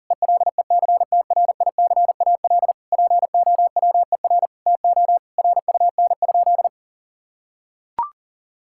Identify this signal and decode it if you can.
Morse code, heard as a keyed tone, about 37 wpm, 700 Hz. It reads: ELECTRICAL POWER TO RUN?
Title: Sentences from Top 1000 Words - Review Part 6 - 40wpm